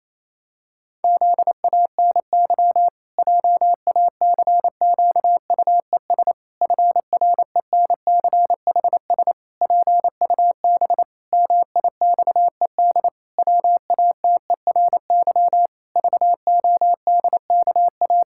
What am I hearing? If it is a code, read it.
Morse code, heard as a keyed tone, about 28 wpm, 700 Hz. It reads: ZANY JACQUES FRENCHS PUB MIXED WATERY VODKA
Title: Pangram Challenge v2 - 28wpm